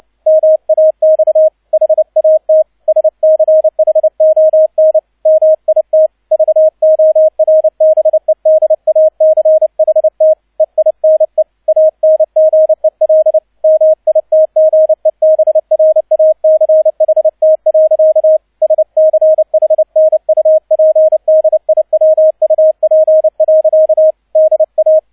Morse code, heard as a keyed tone, about 29 wpm, 630 Hz, with radio noise. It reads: MAX HAT SCHON MIT VORBEDACHT EINE ANGEL MITGEBRACHT. SCHNUPDIWUP. DA
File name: 002 Zweiter Streich 30wpm